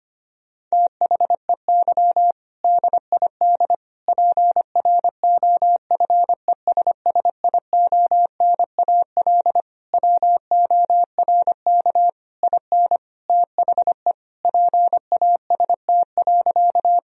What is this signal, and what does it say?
Morse code, heard as a keyed tone, about 25 wpm, 700 Hz. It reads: THEY DID PROFESSIONAL WORK IN THE PAST.
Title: Sentences from Top 1000 Words - Review Part 4 - 25wpm